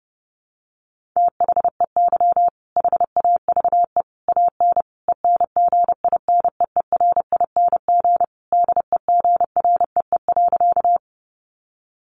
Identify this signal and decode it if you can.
Morse code, heard as a keyed tone, about 28 wpm, 700 Hz. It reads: THEY HAVE AN ENGINEERING DEGREE.
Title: Sentences from Top 1500 Words 30wpm